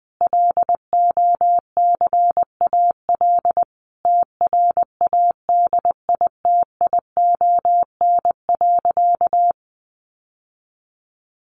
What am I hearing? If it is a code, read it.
Morse code, heard as a keyed tone, about 20 wpm, 700 Hz. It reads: LOCAL TRADITION.
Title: Sentences from Top 1500 Words 20wpm